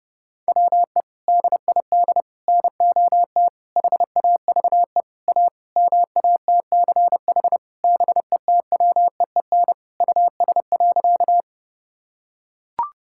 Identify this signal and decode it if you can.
Morse code, heard as a keyed tone, about 30 wpm, 700 Hz. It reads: WE DID NOT HAVE A MATCH BETWEEN US.
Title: Sentences from Top 1600 Words 30wpm